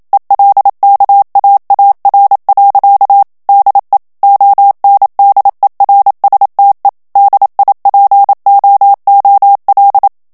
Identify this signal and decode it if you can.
Morse code, heard as a keyed tone, about 27 wpm, 800 Hz. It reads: ELKAAR. DE ONDERSTE DIPOOL
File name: HOLLANDAIS texte clair 1 vitesse 26 mots minute